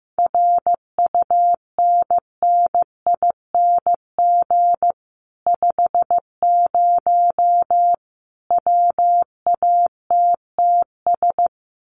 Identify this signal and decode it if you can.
Morse code, heard as a keyed tone, about 15 wpm, 700 Hz. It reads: RUNNING 50 WATTS
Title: Common QSO Phrases 15wpm